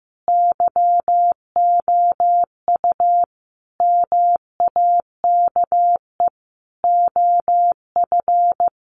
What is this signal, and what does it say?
Morse code, heard as a keyed tone, about 15 wpm, 700 Hz. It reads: YOU MAKE OF
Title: Sentences from Top 100 Words 15wpm